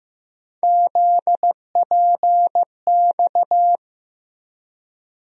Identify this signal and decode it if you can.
Morse code, heard as a keyed tone, about 15 wpm, 700 Hz. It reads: ZPX
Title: USA Call Sign Postfixes - 3 Letters 15wpm